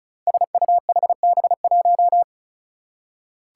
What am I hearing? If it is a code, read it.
Morse code, heard as a keyed tone, about 34 wpm, 700 Hz. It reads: SUHB1
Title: Five Challenging Letter-Number Combinations 35wpm